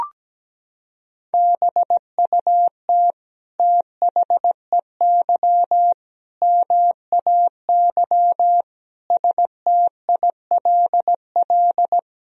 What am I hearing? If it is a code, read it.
Morse code, heard as a keyed tone, about 17 wpm, 700 Hz. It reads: BUT THEY MAY STILL
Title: Sentences from Top 300 Words and Abbreviations 17wpm